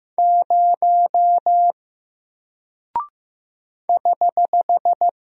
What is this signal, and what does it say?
Morse code, heard as a keyed tone, about 15 wpm, 700 Hz. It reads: 0 <HH>
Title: Single Character (Letter:Number:Punctuation:Prosign) 15wpm